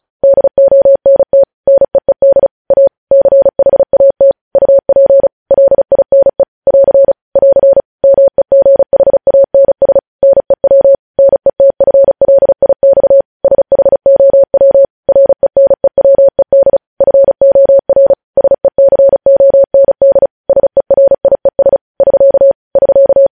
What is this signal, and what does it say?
Morse code, heard as a keyed tone, about 36 wpm, 550 Hz. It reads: DONT NEED A CHAT UP LINE <AR> <AR> MEGHANS NEW NETFLIX SHOW RENEWED FOR SECOND SERIES <SK> <SK>
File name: podcast / News headlines-1x-35x35